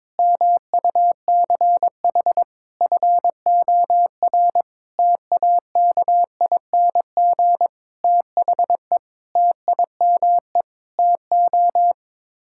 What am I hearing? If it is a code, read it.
Morse code, heard as a keyed tone, about 22 wpm, 700 Hz. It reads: MUCH FOR TAKING THE TIME TO
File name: Sets of 9 Words - 22wpm